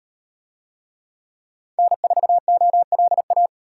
Morse code, heard as a keyed tone, about 35 wpm, 700 Hz. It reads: N4OLA